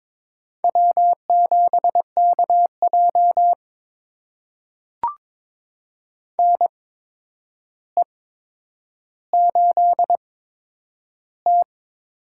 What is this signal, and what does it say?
Morse code, heard as a keyed tone, about 22 wpm, spaced as for 9 wpm, 700 Hz. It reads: W7KJ NE8T
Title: USA Call Signs - 4 Characters - 8x Character Spacing - 22wpm